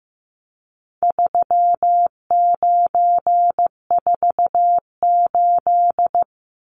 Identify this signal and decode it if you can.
Morse code, heard as a keyed tone, about 15 wpm, 700 Hz. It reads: 3948
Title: Numbers - 4-Digit 15wpm